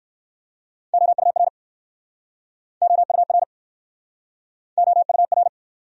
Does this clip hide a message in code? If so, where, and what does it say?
Yes: Morse code from 0.9 to 5.5 s, heard as a keyed tone, about 60 wpm, 700 Hz. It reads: OUR OUR OUR